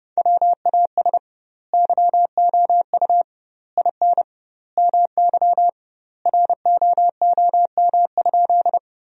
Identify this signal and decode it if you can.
Morse code, heard as a keyed tone, about 32 wpm, 700 Hz. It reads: WAS YOU IN MY ROOM?
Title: 2x Word-Spacing - Sentences from Top 300 Words - 30wpm